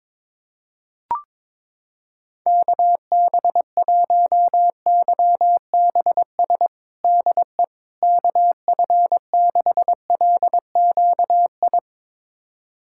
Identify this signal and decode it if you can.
Morse code, heard as a keyed tone, about 22 wpm, 700 Hz. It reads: KB1YBS DE KF6LQI
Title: USA Callsign DE Callsign - 6 Characters - 22wpm